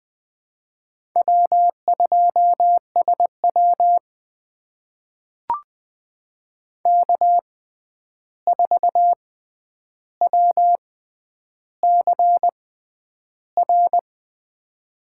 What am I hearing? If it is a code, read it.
Morse code, heard as a keyed tone, about 21 wpm, spaced as for 10 wpm, 700 Hz. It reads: W2SW K4WCR
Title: USA Call Signs - 6x Character Spacing - 20wpm